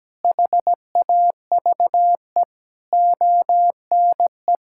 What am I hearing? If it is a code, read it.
Morse code, heard as a keyed tone, about 17 wpm, 700 Hz. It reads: HAVE ONE
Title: Sentences from Top 1600 Words 17wpm